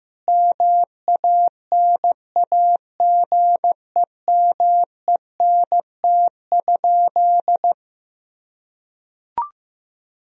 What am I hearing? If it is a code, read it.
Morse code, heard as a keyed tone, about 15 wpm, 700 Hz. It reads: MANAGEMENT?
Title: Sentences from Top 1000 Words - Review Part 4 - 15wpm